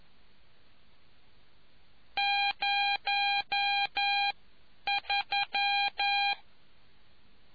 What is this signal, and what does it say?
Morse code, heard as a keyed tone, about 11 wpm, spaced as for 9 wpm, 800 Hz. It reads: 03